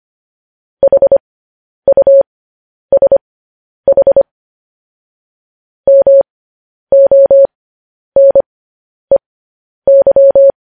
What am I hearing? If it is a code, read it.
Morse code, heard as a keyed tone, about 25 wpm, spaced as for 10 wpm, 550 Hz. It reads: HUSH MONEY